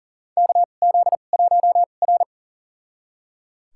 Morse code, heard as a keyed tone, about 40 wpm, spaced as for 29 wpm, 690 Hz. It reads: KZ1R
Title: USA Call Signs - 4 Characters - 2x Character Spacing - 40wpm